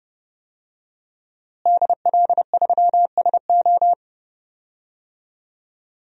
Morse code, heard as a keyed tone, about 30 wpm, 700 Hz. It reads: DL3SO
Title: DXCC-100 Call Signs - 5 Characters - Repeated 3x - 30wpm